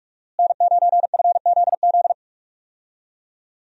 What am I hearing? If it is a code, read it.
Morse code, heard as a keyed tone, about 43 wpm, 700 Hz. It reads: N9FZZ
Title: USA Call Signs - 5 Characters - Repeated 3x - 45wpm